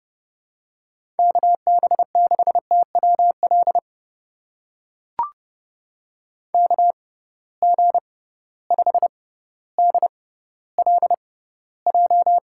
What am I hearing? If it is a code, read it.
Morse code, heard as a keyed tone, about 30 wpm, spaced as for 15 wpm, 700 Hz. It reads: KB6TWL KG5DLJ
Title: USA Call Signs - 6 Characters - 6x Character Spacing - 30wpm